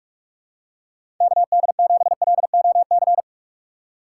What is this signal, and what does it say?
Morse code, heard as a keyed tone, about 46 wpm, 700 Hz. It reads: KD7LOC
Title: USA Call Signs - 6 Characters - 3x Character Spacing - 45wpm